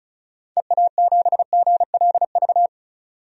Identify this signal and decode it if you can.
Morse code, heard as a keyed tone, about 35 wpm, 700 Hz. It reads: EA7GLV
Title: DXCC-100 Call Signs - 6 Characters - 35wpm